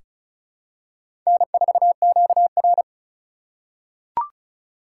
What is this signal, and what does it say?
Morse code, heard as a keyed tone, about 36 wpm, 700 Hz. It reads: N4QR
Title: USA Call Signs - 4 Characters - 3x Character Spacing - 35wpm